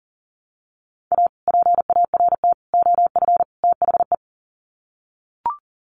Morse code, heard as a keyed tone, about 40 wpm, 700 Hz. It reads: A PART OF THE
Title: Sets of 4 Words 40wpm